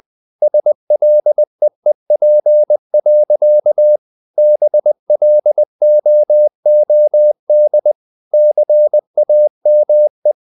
Morse code, heard as a keyed tone, about 20 wpm, 600 Hz. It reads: SLEEP. BLOOD CAME